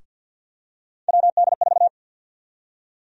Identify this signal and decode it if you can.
Morse code, heard as a keyed tone, about 52 wpm, 700 Hz. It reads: WD4
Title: USA Call Sign Prefixes - Repeated 2x - 50wpm